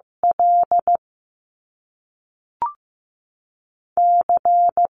Morse code, heard as a keyed tone, about 15 wpm, spaced as for 3 wpm, 700 Hz. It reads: LC
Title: Top Two Letter Combinations in Words 15wpm